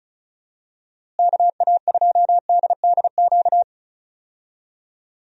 Morse code, heard as a keyed tone, about 36 wpm, 700 Hz. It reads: KA2DDQ
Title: USA Call Signs - 6 Characters - Repeated 3x - 35wpm